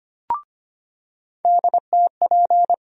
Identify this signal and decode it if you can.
Morse code, heard as a keyed tone, about 25 wpm, 700 Hz. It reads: DTP
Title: USA Call Sign Postfixes - 3 Letters 25wpm